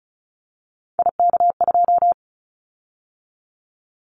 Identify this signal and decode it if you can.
Morse code, heard as a keyed tone, about 34 wpm, 700 Hz. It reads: IK2
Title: Callsign Prefixes for Top 50 DXCC - 35wpm